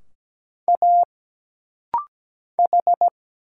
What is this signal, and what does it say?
Morse code, heard as a keyed tone, about 17 wpm, 700 Hz. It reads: A H
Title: Single Letters - Rapid-Fire 17wpm